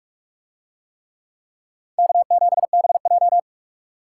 Morse code, heard as a keyed tone, about 43 wpm, 700 Hz. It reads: K7BJ